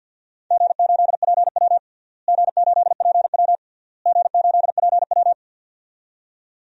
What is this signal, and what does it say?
Morse code, heard as a keyed tone, about 51 wpm, 700 Hz. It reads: G8PW G8PW G8PW
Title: DXCC-100 Call Signs - 4 Characters - Repeated 3x - 50wpm